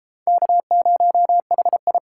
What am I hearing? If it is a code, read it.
Morse code, heard as a keyed tone, about 32 wpm, 700 Hz. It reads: K0HI